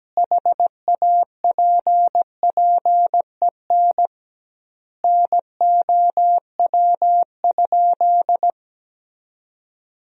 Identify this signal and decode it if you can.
Morse code, heard as a keyed tone, about 17 wpm, 700 Hz. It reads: HAPPEN NOW?
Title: 2x Word-Spacing - Sentences from Top 400 Words - 17wpm